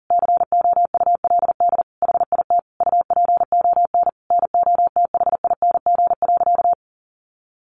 Morse code, heard as a keyed tone, about 37 wpm, 700 Hz. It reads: COULD HIT UPON NOTHING.
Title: Sentences from Top 1200 Words 40wpm